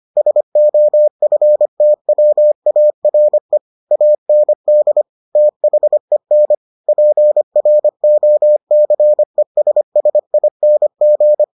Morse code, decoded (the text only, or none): SOFTWARE AND THEN PROCESSING